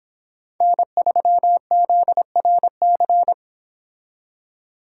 Morse code, heard as a keyed tone, about 26 wpm, 700 Hz. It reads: N3ZRC